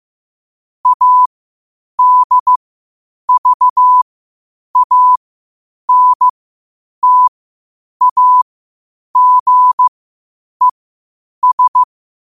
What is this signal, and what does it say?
Morse code, heard as a keyed tone, about 15 wpm, spaced as for 8 wpm, 1000 Hz. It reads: ADVANTAGES
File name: qotdfiles / 8wpm